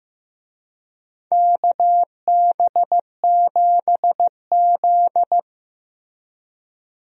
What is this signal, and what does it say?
Morse code, heard as a keyed tone, about 15 wpm, 700 Hz. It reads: KB7Z
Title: USA Call Signs - 4 Characters - Repeated 2x - 15wpm